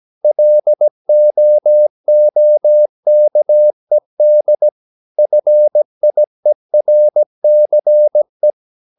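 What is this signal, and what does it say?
Morse code, heard as a keyed tone, about 17 wpm, 600 Hz. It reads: LOOKED FIERCE
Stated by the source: 17 wpm Morse practice sending the English text of a novel.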